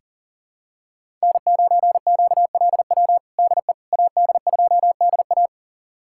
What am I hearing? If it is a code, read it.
Morse code, heard as a keyed tone, about 37 wpm, 700 Hz. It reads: N9QLW DE AD2DA
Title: USA Callsign DE Callsign - 5 Characters - 40wpm